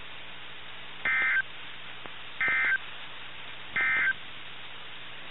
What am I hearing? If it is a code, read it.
Emergency Alert System (SAME) tones, heard as harsh buzzing bursts. It is heard from 0.9 to 4.2 s.
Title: NOAA Weather Radio - KIH61